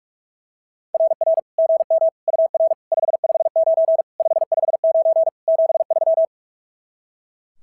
Morse code, heard as a keyed tone, about 42 wpm, 660 Hz. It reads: RR GM UR 559 559 73
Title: SOTA QSO Practice 45wpm